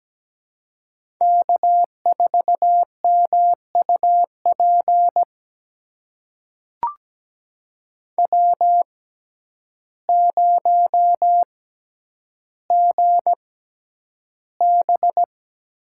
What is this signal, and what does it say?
Morse code, heard as a keyed tone, about 17 wpm, spaced as for 9 wpm, 700 Hz. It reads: K4MUP W0GB
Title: USA Call Signs - 6x Character Spacing - 17wpm